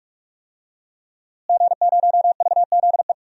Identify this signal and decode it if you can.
Morse code, heard as a keyed tone, about 43 wpm, 700 Hz. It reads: G0VZE
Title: DXCC-100 Call Signs - 5 Characters - Repeated 3x - 45wpm